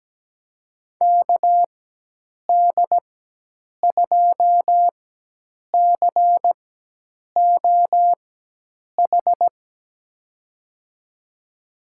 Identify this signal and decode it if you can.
Morse code, heard as a keyed tone, about 17 wpm, 700 Hz. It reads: K D 2 C O H
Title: USA Call Signs - 6 Characters - 4x Character Spacing - 17wpm